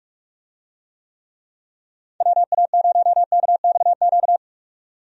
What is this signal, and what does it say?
Morse code, heard as a keyed tone, about 46 wpm, 700 Hz. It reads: WA0KXQ
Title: USA Call Signs - 6 Characters - 45wpm